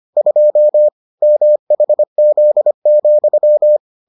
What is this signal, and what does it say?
Morse code, heard as a keyed tone, about 25 wpm, 600 Hz. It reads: 2 MHZ,